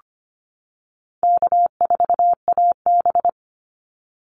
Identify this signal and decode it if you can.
Morse code, heard as a keyed tone, about 25 wpm, 700 Hz. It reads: K4AB
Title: Easy Call Signs - 25wpm